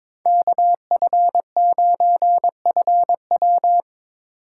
Morse code, heard as a keyed tone, about 22 wpm, 700 Hz. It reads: KF9FW
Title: USA Call Signs - 5 Characters - 2x Character Spacing - 22wpm